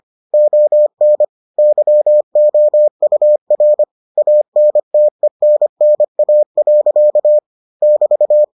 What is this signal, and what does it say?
Morse code, heard as a keyed tone, about 25 wpm, 600 Hz. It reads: ON YOUR ANTENNA. <BT>